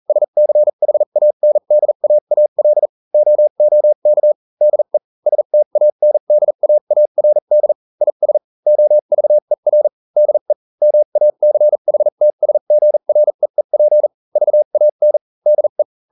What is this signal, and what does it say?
Morse code, heard as a keyed tone, about 40 wpm, spaced as for 35 wpm, 600 Hz. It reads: SCHANDAAL OOK DE STANDAARD IS OVER DE MACHTSGREEP VAN DE